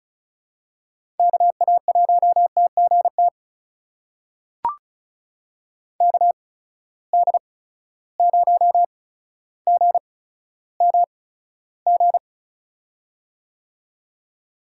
Morse code, heard as a keyed tone, about 36 wpm, spaced as for 15 wpm, 700 Hz. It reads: KA1TGT KD0GMG
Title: USA Call Signs - 6 Characters - 8x Character Spacing - 35wpm